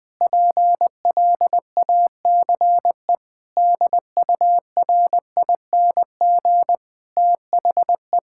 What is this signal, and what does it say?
Morse code, heard as a keyed tone, about 20 wpm, 700 Hz. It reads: PLACE DURING THE
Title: Sentences from Top 1000 Words - Review Part 6 - 20wpm